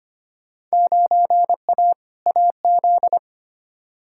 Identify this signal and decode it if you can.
Morse code, heard as a keyed tone, about 25 wpm, 700 Hz. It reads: 9A AZ
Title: Field Day Exchanges - 25wpm